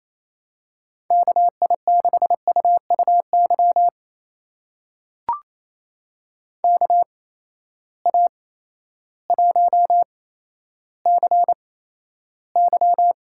Morse code, heard as a keyed tone, about 28 wpm, spaced as for 13 wpm, 700 Hz. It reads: KI6UUY KA1CY
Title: USA Call Signs - 8x Character Spacing - 28wpm